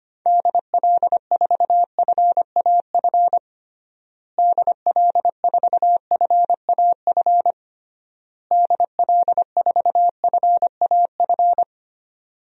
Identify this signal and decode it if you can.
Morse code, heard as a keyed tone, about 25 wpm, 700 Hz. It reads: DL4FAF DL4FAF DL4FAF
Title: DXCC-100 Call Signs - 6 Characters - Repeated 3x - 25wpm